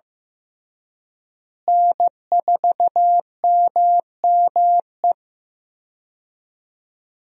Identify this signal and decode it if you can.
Morse code, heard as a keyed tone, about 15 wpm, 700 Hz. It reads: N4MME
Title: USA Call Signs - 5 Characters - Repeated 3x - 15wpm